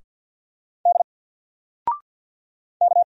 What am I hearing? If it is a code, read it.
Morse code, heard as a keyed tone, about 50 wpm, 700 Hz. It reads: D K
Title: Single Letter-Number 50wpm